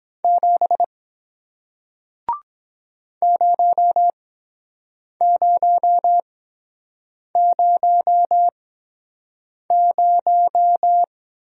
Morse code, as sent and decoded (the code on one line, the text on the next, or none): --... / ----- ----- ----- -----
7 0000